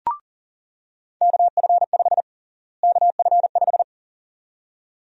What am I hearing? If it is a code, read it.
Morse code, heard as a keyed tone, about 40 wpm, 700 Hz. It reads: KF5 KF5
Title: USA Call Sign Prefixes - Repeated 2x - 40wpm